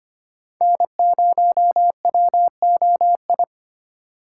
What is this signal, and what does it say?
Morse code, heard as a keyed tone, about 25 wpm, 700 Hz. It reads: N0WOI
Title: K1USN SST QSO Practice - 25wpm